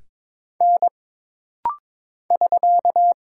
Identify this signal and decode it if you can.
Morse code, heard as a keyed tone, about 22 wpm, 700 Hz. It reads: N <SK>
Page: This rapid-fire timing is geared to push your ICR (Instant Character Recognition) skills.